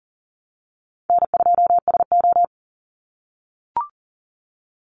Morse code, heard as a keyed tone, about 38 wpm, 700 Hz. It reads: N2SO
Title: Call Signs - 40wpm